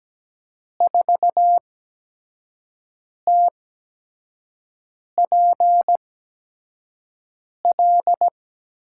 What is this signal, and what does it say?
Morse code, heard as a keyed tone, about 17 wpm, spaced as for 5 wpm, 700 Hz. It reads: 4TPL
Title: USA Call Signs - 8x Character Spacing - 17wpm